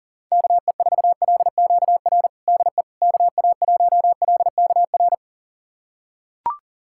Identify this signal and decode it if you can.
Morse code, heard as a keyed tone, about 40 wpm, 700 Hz. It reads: KE4LQR DE KA1LKR
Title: USA Callsign DE Callsign - 6 Characters - 40wpm